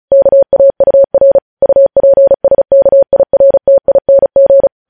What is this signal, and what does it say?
Morse code, heard as a keyed tone, about 36 wpm, 550 Hz. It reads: KAUR UPSKIRTING
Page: News headlines in Morse code at 35 wpm